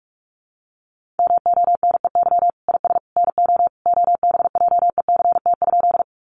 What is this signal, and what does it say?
Morse code, heard as a keyed tone, about 44 wpm, 700 Hz. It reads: MONEY IS NO OBJECT?
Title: Sentences from Top 500 Words 45wpm